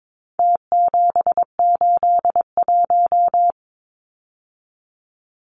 Morse code, heard as a keyed tone, about 22 wpm, 700 Hz. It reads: T781
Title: Numbers - Contest Style - 4-Digit 22wpm